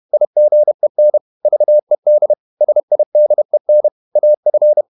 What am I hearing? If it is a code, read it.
Morse code, heard as a keyed tone, about 31 wpm, 600 Hz. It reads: IGEN VED SIDEN AF
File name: Ferie QRQ 31wpm